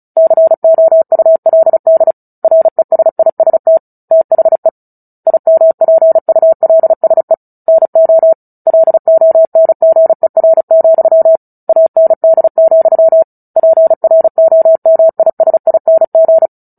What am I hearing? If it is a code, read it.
Morse code, heard as a keyed tone, about 34 wpm, 650 Hz. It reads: COULD RESIST THE IMPULSE NO LONGER, AND, PROMISING
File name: Book Two / 09-Chapter 9 - 35 WPM